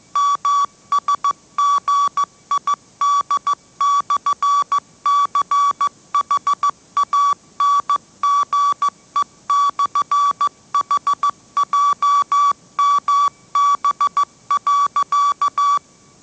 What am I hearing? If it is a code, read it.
Morse code, heard as a keyed tone, about 14 wpm, 1140 Hz. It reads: MSGID/CHANGE/HJMB.
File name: mrx / mrx065 16wpm